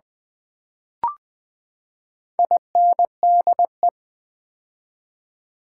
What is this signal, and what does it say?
Morse code, heard as a keyed tone, about 20 wpm, 700 Hz. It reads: INDE
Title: Build A Word - 4 Letters At A Time - 12-Letter Words - 20wpm